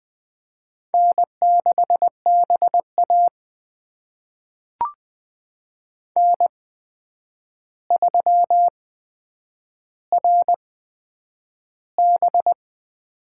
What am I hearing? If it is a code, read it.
Morse code, heard as a keyed tone, about 20 wpm, spaced as for 9 wpm, 700 Hz. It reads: N6BA N3RB
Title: USA Call Signs - 4 Characters - 8x Character Spacing - 20wpm